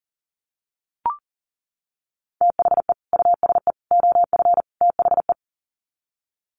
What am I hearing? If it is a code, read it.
Morse code, heard as a keyed tone, about 40 wpm, 700 Hz. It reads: THE USE OF THE
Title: Sets of 4 Words 40wpm